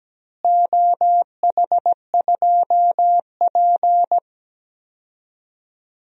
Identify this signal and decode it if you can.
Morse code, heard as a keyed tone, about 17 wpm, 700 Hz. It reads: OH2P
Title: DXCC-100 Call Signs - 4 Characters - Repeated 3x - 17wpm